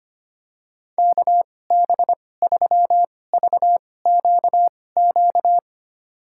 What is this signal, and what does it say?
Morse code, heard as a keyed tone, about 25 wpm, spaced as for 18 wpm, 700 Hz. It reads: KB3VQQ
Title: USA Call Signs - 2x Character Spacing - 25wpm